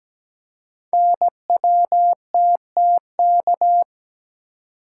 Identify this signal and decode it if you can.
Morse code, heard as a keyed tone, about 17 wpm, 700 Hz. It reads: NWTTK
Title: Code Groups - 5 Letters - 17wpm